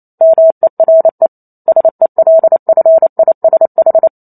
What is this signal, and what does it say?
Morse code, heard as a keyed tone, about 28 wpm, 650 Hz. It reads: MERE SELFISH